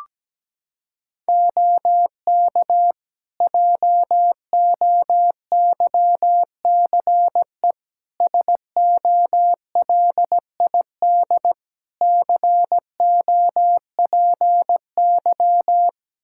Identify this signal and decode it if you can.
Morse code, heard as a keyed tone, about 17 wpm, 700 Hz. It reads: OK JOYCE SOLID COPY
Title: Practice QSO - Exchange 3 - More Info - 17wpm